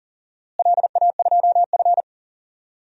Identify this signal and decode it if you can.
Morse code, heard as a keyed tone, about 40 wpm, 700 Hz. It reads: LA2F
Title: DXCC-100 Call Signs - 4 Characters - Repeated 2x - 40wpm